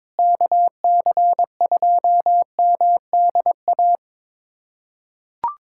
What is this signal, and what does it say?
Morse code, heard as a keyed tone, about 22 wpm, 700 Hz. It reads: KC2MDA
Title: USA Callsign DE Callsign - 6 Characters - 22wpm